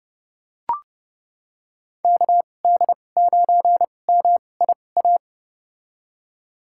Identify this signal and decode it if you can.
Morse code, heard as a keyed tone, about 31 wpm, spaced as for 22 wpm, 700 Hz. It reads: KD9MIA
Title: USA Call Signs - 2x Character Spacing - 30wpm